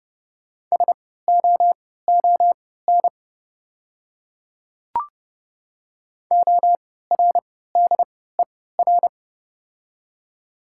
Morse code, heard as a keyed tone, about 31 wpm, spaced as for 17 wpm, 700 Hz. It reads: SOON ORDER
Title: 3x Character Spacing - Top 500 Words - 30wpm